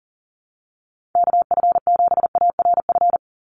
Morse code, heard as a keyed tone, about 40 wpm, 700 Hz. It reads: KF7ARF
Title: USA Call Signs 40wpm